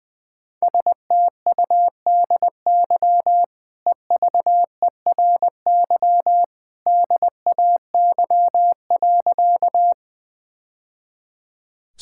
Morse code, heard as a keyed tone, about 20 wpm, 700 Hz. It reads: STUDY EVERY DAY.
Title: Easy Sentences with Hints - 20wpm